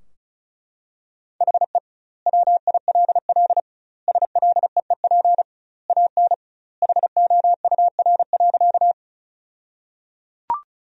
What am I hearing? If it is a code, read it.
Morse code, heard as a keyed tone, about 36 wpm, 700 Hz. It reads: HE WILL SLEEP AN HOUR.
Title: 2x Word-Spacing - Sentences from Top 500 Words - 35wpm